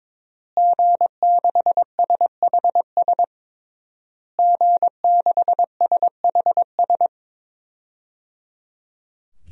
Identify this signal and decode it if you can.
Morse code, heard as a keyed tone, about 22 wpm, 700 Hz. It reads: G6SHS G6SHS
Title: DXCC-100 Call Signs - 5 Characters - Repeated 3x - 22wpm